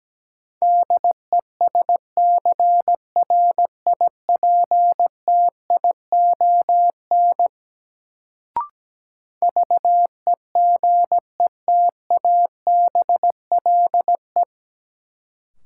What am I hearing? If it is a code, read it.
Morse code, heard as a keyed tone, about 17 wpm, 700 Hz. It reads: DESCRIPTION VEGETABLE
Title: Sentences from Top 1300 Words - New Word Drill - 17wpm